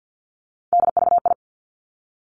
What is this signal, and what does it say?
Morse code, heard as a keyed tone, about 50 wpm, 700 Hz. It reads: DVI
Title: Three Challenging Letter-Number Combinations 50wpm